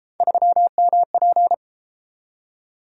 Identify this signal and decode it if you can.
Morse code, heard as a keyed tone, about 30 wpm, 700 Hz. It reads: 3MP